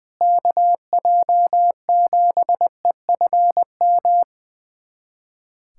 Morse code, heard as a keyed tone, about 20 wpm, 700 Hz. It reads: KJ7EFM
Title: USA Callsign DE Callsign - 6 Characters - 20wpm